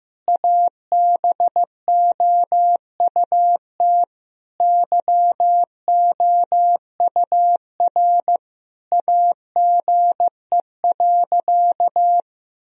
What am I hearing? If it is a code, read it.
Morse code, heard as a keyed tone, about 15 wpm, 700 Hz. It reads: ABOUT YOUR AGE.
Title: Sentences from Top 1000 Words - Review Part 3 - 15wpm